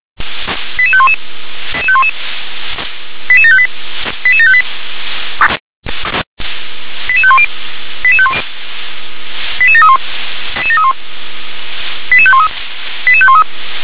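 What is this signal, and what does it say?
Five-tone selective call (ZVEI), heard as quick runs of five tones, tones of about 70 ms. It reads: 88511 88758 88758 88510 88731 88513 88513
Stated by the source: ZVEI five-tone call 88631 received by SDR.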